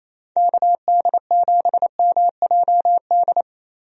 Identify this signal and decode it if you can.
Morse code, heard as a keyed tone, about 28 wpm, 700 Hz. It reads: KD7MJD
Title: USA Call Signs - 6 Characters - Repeated 3x - 28wpm